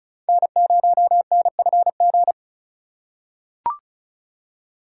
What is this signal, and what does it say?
Morse code, heard as a keyed tone, about 36 wpm, 700 Hz. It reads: N0NFG
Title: USA Call Signs - 5 Characters - 2x Character Spacing - 35wpm